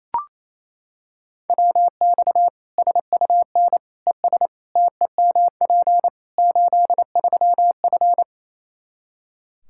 Morse code, heard as a keyed tone, about 27 wpm, 700 Hz. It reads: WX SUN ES TEMP 83F
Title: K1USN SST QSO Practice 28wpm